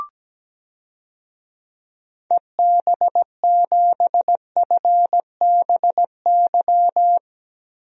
Morse code, heard as a keyed tone, about 17 wpm, 700 Hz. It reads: EB7FBY
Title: DXCC-100 Call Signs - 6 Characters - Repeated 3x - 17wpm